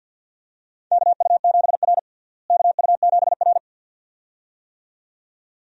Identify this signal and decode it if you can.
Morse code, heard as a keyed tone, about 52 wpm, 700 Hz. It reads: KU7R KU7R
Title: USA Call Signs - 4 Characters - Repeated 2x - 50wpm